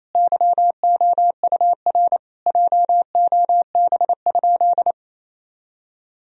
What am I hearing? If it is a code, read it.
Morse code, heard as a keyed tone, about 27 wpm, 700 Hz. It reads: YOUR JOB?
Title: Sentences from Top 1000 Words - Review Part 3 - 28wpm